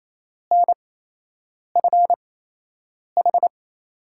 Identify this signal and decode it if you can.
Morse code, heard as a keyed tone, about 28 wpm, spaced as for 8 wpm, 700 Hz. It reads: NFH
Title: USA Call Signs - 5 Characters - 8x Character Spacing - 28wpm